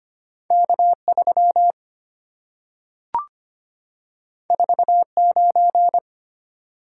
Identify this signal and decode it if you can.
Morse code, heard as a keyed tone, about 25 wpm, 700 Hz. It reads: K3 49
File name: Two Letter-Number Combinations - 25wpm